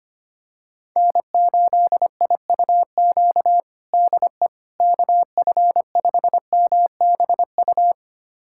Morse code, heard as a keyed tone, about 25 wpm, 700 Hz. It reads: N8IUQ DE KF5MBU
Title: USA Callsign DE Callsign 25wpm